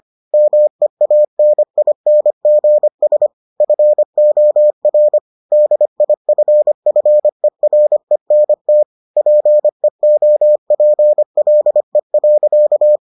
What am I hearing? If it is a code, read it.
Morse code, heard as a keyed tone, about 25 wpm, 600 Hz. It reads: MEANINGS FOR DIFFERENT PEOPLE.